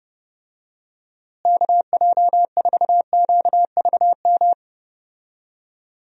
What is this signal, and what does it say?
Morse code, heard as a keyed tone, about 30 wpm, 700 Hz. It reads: KJ4QVM
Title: USA Call Signs - 6 Characters - Repeated 2x - 30wpm